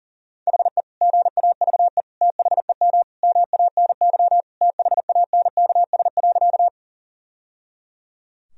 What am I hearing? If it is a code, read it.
Morse code, heard as a keyed tone, about 40 wpm, 700 Hz. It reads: HE GAVE THEM MANY THANKS.
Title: Sentences from Top 1000 Words - Review Part 3 - 40wpm